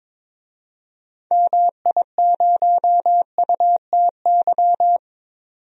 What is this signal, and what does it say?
Morse code, heard as a keyed tone, about 22 wpm, 700 Hz. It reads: MI0UTY
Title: DXCC-100 Call Signs - 6 Characters - 22wpm